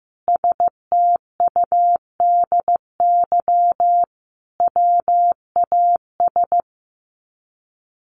Morse code, heard as a keyed tone, about 15 wpm, 700 Hz. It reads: STUDY WAS
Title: Sets of 5 Words 15wpm